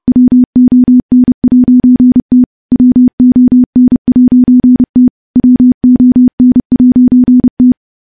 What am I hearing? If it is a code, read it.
Morse code, heard as a keyed tone, about 30 wpm, 250 Hz. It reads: WON'T WON'T WON'T
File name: morse / morse-training-30WPM